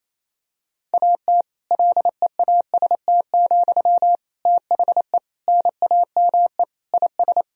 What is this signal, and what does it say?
Morse code, heard as a keyed tone, about 28 wpm, 700 Hz. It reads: AT LEAST, THE NAME IS
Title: Sentences from Top 1000 Words - Review Part 4 - 28wpm